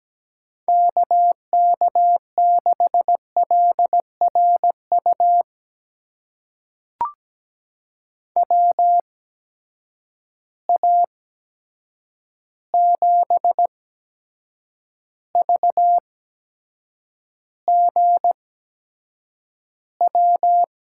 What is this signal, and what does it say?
Morse code, heard as a keyed tone, about 17 wpm, spaced as for 7 wpm, 700 Hz. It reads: KK6LRU WA7VGW
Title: USA Call Signs - 6 Characters - 8x Character Spacing - 17wpm